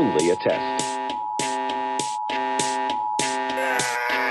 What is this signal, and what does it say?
Emergency Alert System (SAME) tones (the 853 and 960 Hz attention tone), with speech. It is heard from 0.8 to 3.2 s.